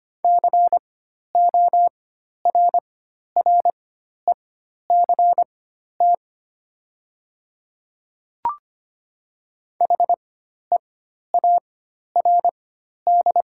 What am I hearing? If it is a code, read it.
Morse code, heard as a keyed tone, about 25 wpm, spaced as for 12 wpm, 700 Hz. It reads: CORRECT HEARD
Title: 4x Character Spacing - Top 500 Words - 25wpm